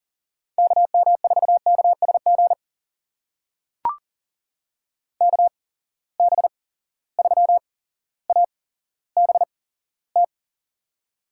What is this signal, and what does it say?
Morse code, heard as a keyed tone, about 40 wpm, spaced as for 17 wpm, 700 Hz. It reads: KM4KSG KB3ABT
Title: USA Call Signs - 6 Characters - 8x Character Spacing - 40wpm